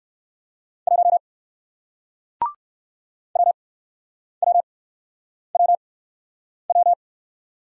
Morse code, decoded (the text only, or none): 1 WWWW